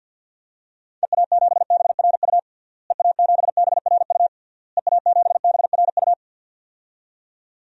Morse code, heard as a keyed tone, about 48 wpm, 700 Hz. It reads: EA7BRU EA7BRU EA7BRU
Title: DXCC-100 Call Signs - 6 Characters - Repeated 3x - 50wpm